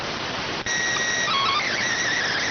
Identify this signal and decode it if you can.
SSTV (slow-scan television), PD 290.